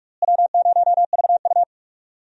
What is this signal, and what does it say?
Morse code, heard as a keyed tone, about 44 wpm, 700 Hz. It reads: W0VU
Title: USA Call Signs - 4 Characters - Repeated 2x - 45wpm